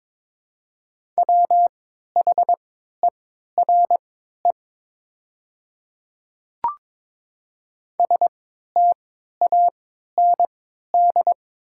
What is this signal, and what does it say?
Morse code, heard as a keyed tone, about 22 wpm, spaced as for 13 wpm, 700 Hz. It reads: WHERE STAND
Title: 3x Character Spacing - Top 300 Words - 22wpm